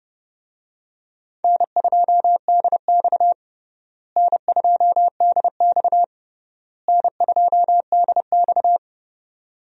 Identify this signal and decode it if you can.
Morse code, heard as a keyed tone, about 32 wpm, 700 Hz. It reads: N2DX N2DX N2DX
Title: USA Call Signs - 4 Characters - Repeated 3x - 30wpm